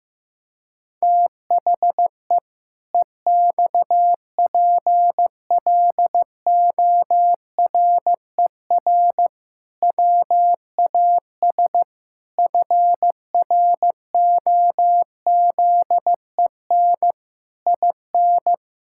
Morse code, heard as a keyed tone, about 15 wpm, 700 Hz. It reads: THE EXPLORER WAS FROZEN IN
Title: Pangram Challenge v2 - 15wpm